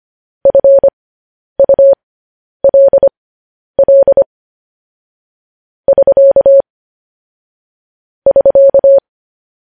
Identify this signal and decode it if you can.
Morse code, heard as a keyed tone, about 25 wpm, spaced as for 10 wpm, 550 Hz. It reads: FULL <SK> <SK>
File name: podcast / News headlines-1x-25x10